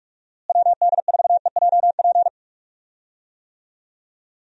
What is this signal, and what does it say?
Morse code, heard as a keyed tone, about 46 wpm, 700 Hz. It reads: WD4EJP
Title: USA Call Signs - 6 Characters - Repeated 3x - 45wpm